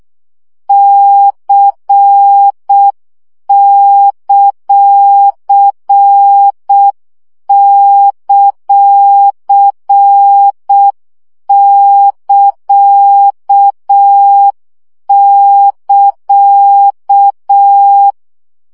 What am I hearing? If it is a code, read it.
Morse code, heard as a keyed tone, about 6 wpm, 800 Hz. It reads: C;;<KA><KA>
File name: CW gyakorlas c ct pontosvesszo 5 baud